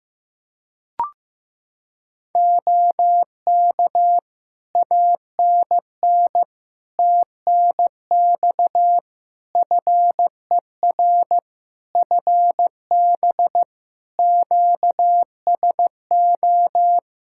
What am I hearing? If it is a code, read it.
Morse code, heard as a keyed tone, about 15 wpm, 700 Hz. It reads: OK ANN TNX FER FB QSO
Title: Practice QSO - Exchange 4 - Ending - 15wpm